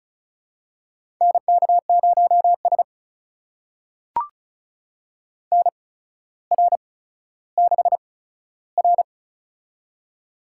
Morse code, heard as a keyed tone, about 36 wpm, spaced as for 15 wpm, 700 Hz. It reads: NK0S NR6R
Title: USA Call Signs - 4 Characters - 8x Character Spacing - 35wpm